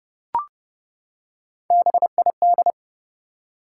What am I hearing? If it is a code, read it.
Morse code, heard as a keyed tone, about 30 wpm, 700 Hz. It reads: BID
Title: Three Letter Words - 30wpm